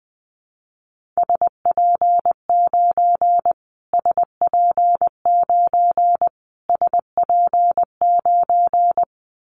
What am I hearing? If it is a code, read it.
Morse code, heard as a keyed tone, about 20 wpm, 700 Hz. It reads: SP9 SP9 SP9
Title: Callsign Prefixes for Top 25 DXCC - Repeated 3x - 20wpm